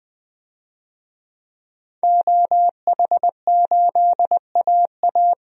Morse code, heard as a keyed tone, about 20 wpm, 700 Hz. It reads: OH8AA
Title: DXCC-100 Call Signs - 5 Characters - 20wpm